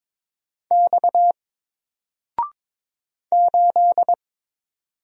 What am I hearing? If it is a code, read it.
Morse code, heard as a keyed tone, about 22 wpm, 700 Hz. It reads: X 8
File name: Single-Letter-Number-22wpm